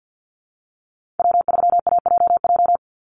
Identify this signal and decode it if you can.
Morse code, heard as a keyed tone, about 50 wpm, 700 Hz. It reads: W3AJJ
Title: USA Call Signs 50wpm